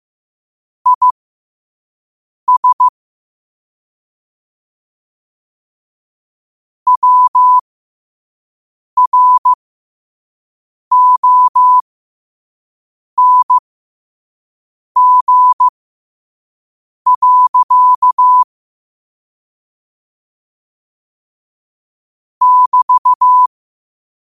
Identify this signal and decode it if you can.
Morse code, heard as a keyed tone, about 15 wpm, spaced as for 5 wpm, 1000 Hz. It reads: IS WRONG. <BT>
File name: qotdfiles / 5wpm